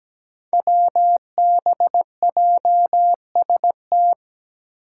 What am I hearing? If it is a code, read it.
Morse code, heard as a keyed tone, about 17 wpm, 700 Hz. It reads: WBJST
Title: Five Challenging Letter-Number Combinations 17wpm